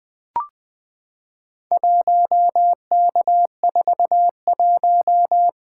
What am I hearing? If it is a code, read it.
Morse code, heard as a keyed tone, about 21 wpm, 700 Hz. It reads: 1K41
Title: Four Letter-Number Combinations - 20wpm